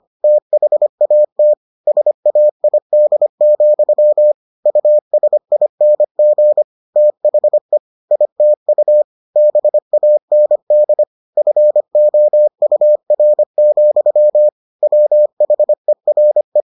Morse code, heard as a keyed tone, about 25 wpm, 600 Hz. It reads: THAT SAID, USING THE ITU BAND FOUR, WHERE